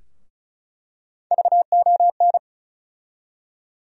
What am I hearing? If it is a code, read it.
Morse code, heard as a keyed tone, about 36 wpm, 700 Hz. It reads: VON
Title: USA Call Sign Postfixes - 3 Letters 35wpm